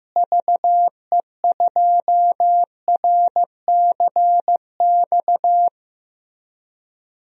Morse code, heard as a keyed tone, about 15 wpm, 700 Hz. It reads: VE2RCX
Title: DXCC-100 Call Signs - 6 Characters - Repeated 3x - 15wpm